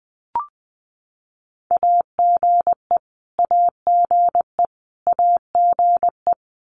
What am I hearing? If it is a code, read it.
Morse code, heard as a keyed tone, about 20 wpm, 700 Hz. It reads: AGE AGE AGE
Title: Top 500 Words Repeated 20wpm